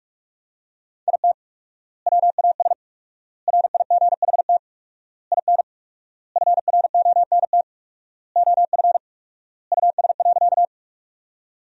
Morse code, heard as a keyed tone, about 41 wpm, 700 Hz. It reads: IT WAS RIGHT IN FRONT OF US.
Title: 4x Word-Spacing - Sentences from Top 500 Words - 45wpm